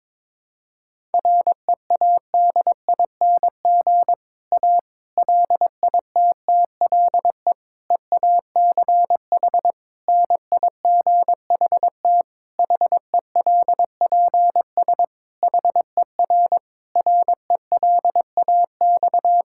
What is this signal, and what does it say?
Morse code, heard as a keyed tone, about 22 wpm, 700 Hz. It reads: READING A LITTLE EACH NIGHT HELPS HER RELAX
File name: Difficult Sentences with Hints - 22wpm